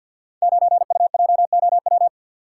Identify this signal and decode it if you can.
Morse code, heard as a keyed tone, about 50 wpm, 700 Hz. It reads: 9UJOW
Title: Code Groups - 5 Letter:Numbers - 50wpm